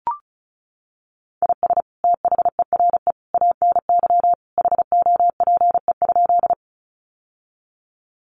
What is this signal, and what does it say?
Morse code, heard as a keyed tone, about 34 wpm, 700 Hz. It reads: IS THERE ANY HOPE?
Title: Sentences from Top 1000 Words 35wpm